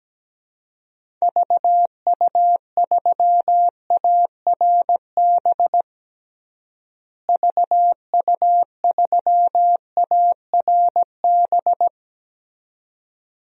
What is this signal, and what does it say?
Morse code, heard as a keyed tone, about 17 wpm, 700 Hz. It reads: VU3ARB VU3ARB
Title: DXCC-100 Call Signs - 6 Characters - Repeated 2x - 17wpm